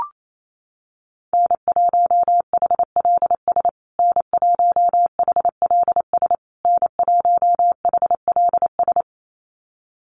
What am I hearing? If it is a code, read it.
Morse code, heard as a keyed tone, about 28 wpm, 700 Hz. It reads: N1HLS N1HLS N1HLS
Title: USA Call Signs Repeated 3x 28wpm